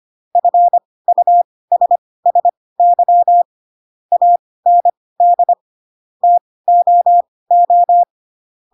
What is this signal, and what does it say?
Morse code, heard as a keyed tone, about 25 wpm, spaced as for 18 wpm, 700 Hz. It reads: FUSSY AND TOO
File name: Pinocchio / Pinocchio ch7 12 25 ik1hge